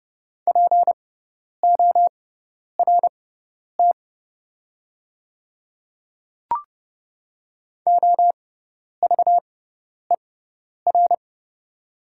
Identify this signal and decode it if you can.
Morse code, heard as a keyed tone, about 30 wpm, spaced as for 10 wpm, 700 Hz. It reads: PORT OVER
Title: 6x Character Spacing - Top 200 Words - 30wpm